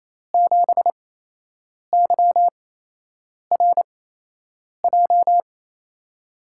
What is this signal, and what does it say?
Morse code, heard as a keyed tone, about 27 wpm, spaced as for 8 wpm, 700 Hz. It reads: 7YRJ
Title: USA Call Signs - 5 Characters - 8x Character Spacing - 28wpm